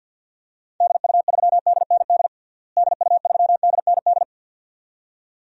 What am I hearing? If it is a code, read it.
Morse code, heard as a keyed tone, about 47 wpm, 700 Hz. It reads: DU3DND DU3DND
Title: DXCC-100 Call Signs - 6 Characters - Repeated 2x - 50wpm